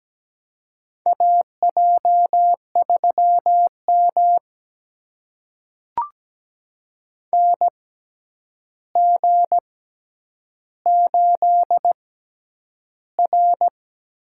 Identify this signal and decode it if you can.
Morse code, heard as a keyed tone, about 17 wpm, spaced as for 9 wpm, 700 Hz. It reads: AJ3M NG8R
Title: USA Call Signs - 6x Character Spacing - 17wpm